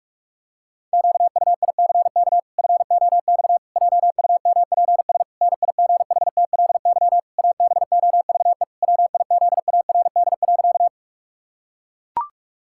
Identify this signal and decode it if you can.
Morse code, heard as a keyed tone, about 44 wpm, 700 Hz. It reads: QUICK FOX JUMPS NIGHTLY ABOVE WIZARD.
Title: Pangram Challenge v2 - 45wpm